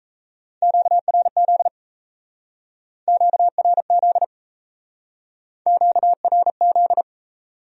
Morse code, heard as a keyed tone, about 35 wpm, 700 Hz. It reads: QRZ QRZ QRZ
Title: Speed-Racing Q-Codes - 28wpm